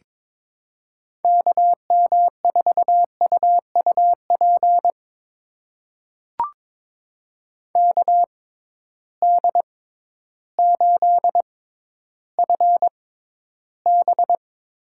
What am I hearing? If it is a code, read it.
Morse code, heard as a keyed tone, about 22 wpm, spaced as for 12 wpm, 700 Hz. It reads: KM4UUP KD8FB
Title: USA Call Signs - 6 Characters - 6x Character Spacing - 22wpm